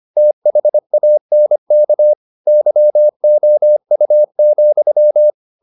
Morse code, heard as a keyed tone, about 25 wpm, 600 Hz. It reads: THANK YOU,
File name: the old man and the sea part 1 25wpm